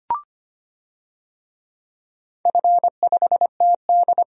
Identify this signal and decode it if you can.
Morse code, heard as a keyed tone, about 25 wpm, 700 Hz. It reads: F5TD
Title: DXCC-100 Call Signs - 4 Characters - 25wpm